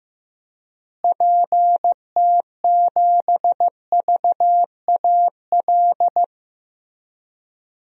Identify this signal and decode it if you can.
Morse code, heard as a keyed tone, about 15 wpm, 700 Hz. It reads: PT7VAL
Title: DXCC-100 Call Signs - 6 Characters - Repeated 3x - 15wpm